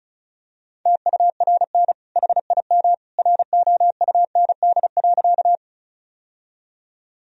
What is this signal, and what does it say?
Morse code, heard as a keyed tone, about 34 wpm, 700 Hz. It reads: TURN HIM ROUND.
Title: Sentences from Top 1000 Words - Review Part 1 - 35wpm